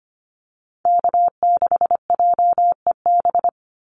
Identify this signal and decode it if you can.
Morse code, heard as a keyed tone, about 25 wpm, 700 Hz. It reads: K6JEB
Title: Call Signs - 25wpm